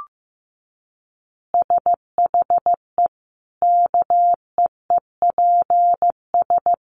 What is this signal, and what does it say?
Morse code, heard as a keyed tone, about 15 wpm, 700 Hz. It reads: SHE KEEPS
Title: Easy Phrases 15wpm